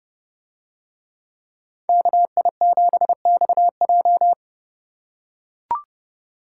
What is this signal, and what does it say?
Morse code, heard as a keyed tone, about 28 wpm, 700 Hz. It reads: KI7XJ